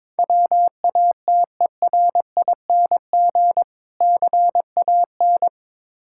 Morse code, heard as a keyed tone, about 22 wpm, 700 Hz. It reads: WATERING CAN
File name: practice / 22wpm / Difficult Words with Hints - 22wpm